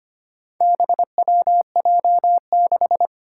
Morse code, heard as a keyed tone, about 25 wpm, 700 Hz. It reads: BWJ6
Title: Four Challenging Letter-Number Combinations 25wpm